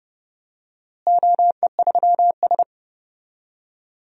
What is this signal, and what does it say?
Morse code, heard as a keyed tone, about 30 wpm, 700 Hz. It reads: OE3S
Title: DXCC-100 Call Signs - 4 Characters - Repeated 3x - 30wpm